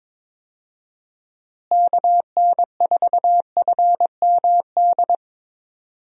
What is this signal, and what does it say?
Morse code, heard as a keyed tone, about 22 wpm, 700 Hz. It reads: KN4FMD
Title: USA Call Signs - 6 Characters - Repeated 3x - 22wpm